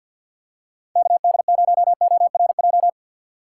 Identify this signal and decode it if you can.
Morse code, heard as a keyed tone, about 48 wpm, 700 Hz. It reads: KD0ORJ
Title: USA Call Signs - 6 Characters - 4x Character Spacing - 50wpm